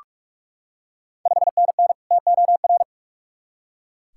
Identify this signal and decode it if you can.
Morse code, heard as a keyed tone, about 44 wpm, 690 Hz. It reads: 5NN TOR
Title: Ontario QSO Party Exchange 45wpm